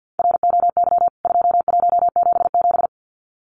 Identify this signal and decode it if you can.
Morse code, heard as a keyed tone, about 49 wpm, 700 Hz. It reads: ROY 2177
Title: CWT Exchange 50wpm